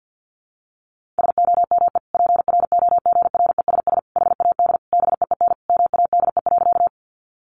Morse code, heard as a keyed tone, about 50 wpm, 700 Hz. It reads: SOME PROGRESS HAD BEEN MADE.
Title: Sentences from Top 1400 Words 50wpm